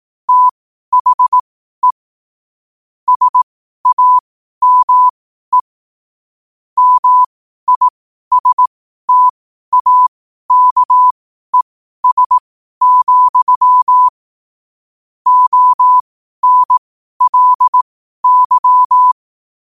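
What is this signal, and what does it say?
Morse code, heard as a keyed tone, about 18 wpm, spaced as for 12 wpm, 1000 Hz. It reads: THE SAME MISTAKES, ONLY
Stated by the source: Morse code at 12 words per minute.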